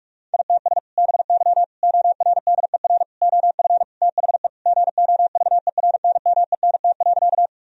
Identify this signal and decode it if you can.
Morse code, heard as a keyed tone, about 41 wpm, 700 Hz. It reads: ITS BY ORDER OF THE GOVERNMENT.
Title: Sentences from Top 1000 Words - Review Part 3 - 45wpm